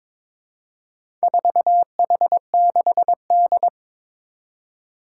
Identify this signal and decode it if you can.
Morse code, heard as a keyed tone, about 22 wpm, 700 Hz. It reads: 4H6D
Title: Four Challenging Letter-Number Combinations 22wpm